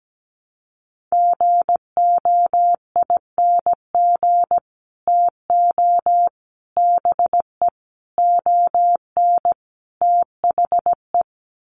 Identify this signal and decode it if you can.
Morse code, heard as a keyed tone, about 17 wpm, 700 Hz. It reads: GOING TO BE ON THE
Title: Sets of 5 Words Encore2 17wpm